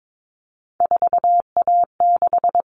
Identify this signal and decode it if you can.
Morse code, heard as a keyed tone, about 22 wpm, 700 Hz. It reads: 4A6
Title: All Cut Numbers - 3-Digit 22wpm